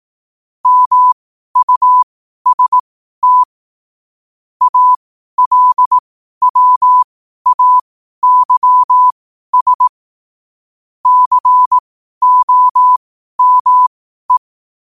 Morse code, heard as a keyed tone, about 18 wpm, spaced as for 12 wpm, 1000 Hz. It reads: MUST ALWAYS COME
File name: qotdfiles / 12wpm